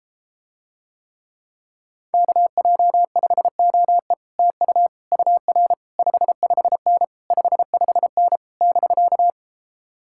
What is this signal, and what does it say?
Morse code, heard as a keyed tone, about 33 wpm, 700 Hz. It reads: KJ5OE TU UR 55N 55N <BK>